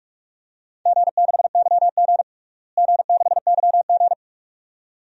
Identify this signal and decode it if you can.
Morse code, heard as a keyed tone, about 46 wpm, 700 Hz. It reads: G6YG G6YG
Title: DXCC-100 Call Signs - 4 Characters - Repeated 2x - 45wpm